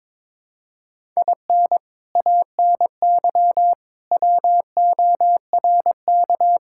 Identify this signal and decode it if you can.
Morse code, heard as a keyed tone, about 22 wpm, 700 Hz. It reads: IN ANY WORK
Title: Sentences from Top 1000 Words - Review Part 6 - 22wpm